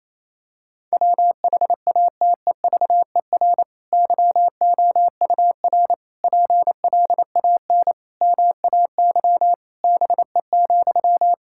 Morse code, heard as a keyed tone, about 28 wpm, 700 Hz. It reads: WHATEVER YOUR PLAN MAY BE,
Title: Sentences from Top 1600 Words 28wpm